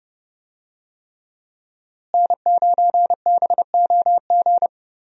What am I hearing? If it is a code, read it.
Morse code, heard as a keyed tone, about 32 wpm, 700 Hz. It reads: N9BOG
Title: USA Call Signs - 5 Characters - Repeated 3x - 30wpm